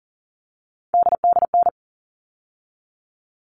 Morse code, heard as a keyed tone, about 40 wpm, 700 Hz. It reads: DDN
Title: All Cut Numbers - 3-Digit 40wpm